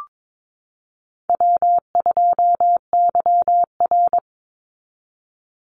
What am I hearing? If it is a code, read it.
Morse code, heard as a keyed tone, about 22 wpm, 700 Hz. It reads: W2YR
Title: Easy Call Signs Repeated 3x - 22wpm